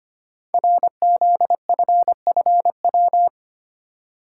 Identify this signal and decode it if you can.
Morse code, heard as a keyed tone, about 25 wpm, 700 Hz. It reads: RZFFW
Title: Code Groups - 5 Letters - 25wpm